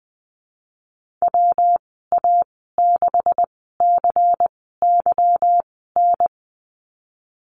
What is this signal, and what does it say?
Morse code, heard as a keyed tone, about 20 wpm, spaced as for 14 wpm, 700 Hz. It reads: WA6CYN
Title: 200 Lesson 32g - Letter k - Call Signs 20wpm